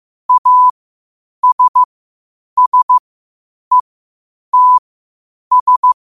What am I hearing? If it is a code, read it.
Morse code, heard as a keyed tone, about 15 wpm, spaced as for 8 wpm, 1000 Hz. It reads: ASSETS
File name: qotdfiles / 8wpm